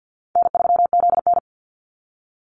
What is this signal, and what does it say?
Morse code, heard as a keyed tone, about 50 wpm, 700 Hz. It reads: N3ZN
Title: Easy Call Signs Repeated 3x - 50wpm